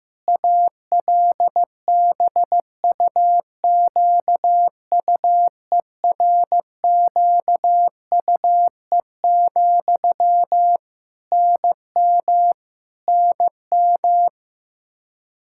Morse code, heard as a keyed tone, about 15 wpm, 700 Hz. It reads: ALBUQUERQUE, NM NM
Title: K1USN SST QSO Practice 15wpm